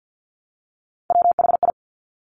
Morse code, heard as a keyed tone, about 50 wpm, 700 Hz. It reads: WHI
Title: Top Three Letter Combinations in Words 50wpm